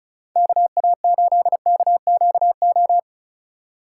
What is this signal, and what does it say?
Morse code, heard as a keyed tone, about 36 wpm, 700 Hz. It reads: KA8KQO
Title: USA Call Signs - 6 Characters - 4x Character Spacing - 35wpm